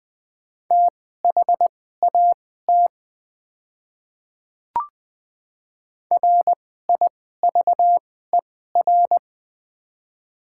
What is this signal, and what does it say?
Morse code, heard as a keyed tone, about 20 wpm, spaced as for 14 wpm, 700 Hz. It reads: THAT RIVER
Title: 2x Character Spacing - Top 500 Words - 20wpm